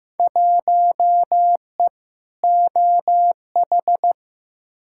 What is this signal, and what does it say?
Morse code, heard as a keyed tone, about 15 wpm, 700 Hz. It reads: 1E OH
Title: Field Day Exchanges - 15wpm